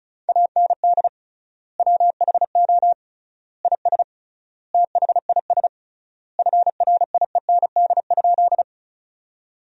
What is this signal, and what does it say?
Morse code, heard as a keyed tone, about 36 wpm, 700 Hz. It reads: AND WHO IS THIS FRIEND?
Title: 3x Word-Spacing - Sentences from Top 300 Words - 35wpm